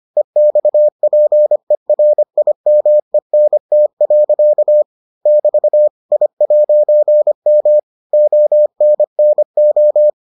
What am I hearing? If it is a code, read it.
Morse code, heard as a keyed tone, about 25 wpm, 600 Hz. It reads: EXPERIMENT. <BT> I'M ONNO